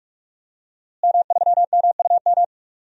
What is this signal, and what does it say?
Morse code, heard as a keyed tone, about 46 wpm, 700 Hz. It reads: M3MUM
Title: DXCC-100 Call Signs - 5 Characters - Repeated 2x - 45wpm